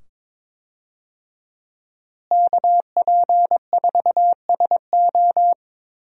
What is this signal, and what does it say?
Morse code, heard as a keyed tone, about 22 wpm, 700 Hz. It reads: KP4SO
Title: DXCC-100 Call Signs - 5 Characters - Repeated 2x - 22wpm